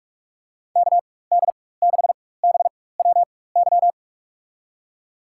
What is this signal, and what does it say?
Morse code, heard as a keyed tone, about 46 wpm, spaced as for 21 wpm, 700 Hz. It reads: KD6BWY
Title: USA Call Signs - 6 Characters - 4x Character Spacing - 45wpm